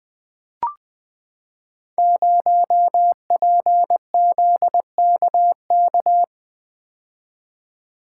Morse code, heard as a keyed tone, about 20 wpm, 700 Hz. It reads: 0PZKK
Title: Code Groups - 5 Letter:Numbers - 20wpm